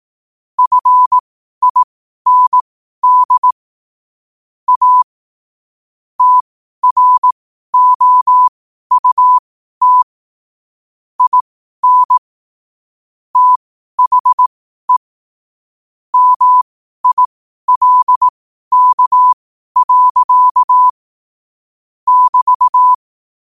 Morse code, heard as a keyed tone, about 18 wpm, spaced as for 12 wpm, 1000 Hz. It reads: FIND A TROUT IN THE MILK. <BT>
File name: qotdfiles / 12wpm